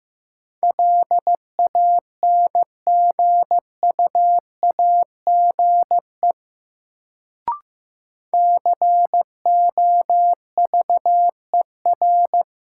Morse code, heard as a keyed tone, about 15 wpm, 700 Hz. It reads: LANGUAGE COVER
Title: Sentences from Top 500 Words - New Word Drill - 15wpm